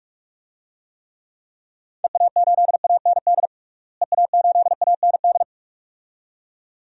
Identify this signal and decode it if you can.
Morse code, heard as a keyed tone, about 46 wpm, 690 Hz. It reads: EA8AND EA8AND
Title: DXCC-100 Call Signs - 6 Characters - Repeated 2x - 45wpm